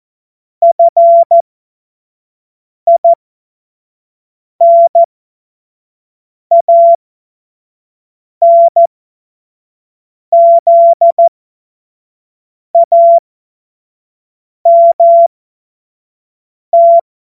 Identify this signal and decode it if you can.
Morse code, heard as a keyed tone, about 14 wpm, spaced as for 5 wpm, 680 Hz. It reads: FINANZAMT
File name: Witz-21-Oktober-2025-25-F70